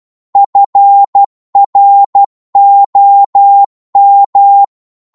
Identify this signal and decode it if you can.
Morse code, heard as a keyed tone, about 12 wpm, 800 Hz. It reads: FROM
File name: cwmp3 / psg-1-12wpm